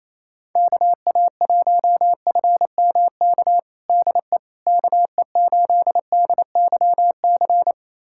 Morse code, heard as a keyed tone, about 28 wpm, 700 Hz. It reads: KA1FMK DE KE8DYC
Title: USA Callsign DE Callsign - 6 Characters - 28wpm